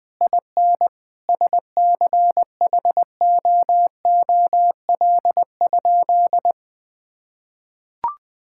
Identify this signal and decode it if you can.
Morse code, heard as a keyed tone, about 20 wpm, 700 Hz. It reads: IN SCHOOL?
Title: Sentences from Top 1000 Words - Review Part 5 - 20wpm